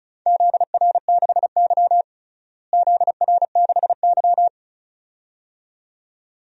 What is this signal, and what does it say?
Morse code, heard as a keyed tone, about 36 wpm, 700 Hz. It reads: ZR6Y ZR6Y
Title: DXCC-100 Call Signs - 4 Characters - Repeated 2x - 35wpm